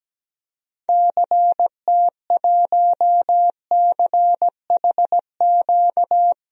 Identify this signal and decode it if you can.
Morse code, heard as a keyed tone, about 17 wpm, 700 Hz. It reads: CT1CHQ
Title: DXCC-100 Call Signs - 6 Characters - 17wpm